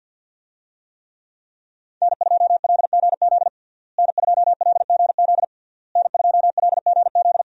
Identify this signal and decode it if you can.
Morse code, heard as a keyed tone, about 51 wpm, 700 Hz. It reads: N2LGZ N2LGZ N2LGZ
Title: USA Call Signs - 5 Characters - Repeated 3x - 50wpm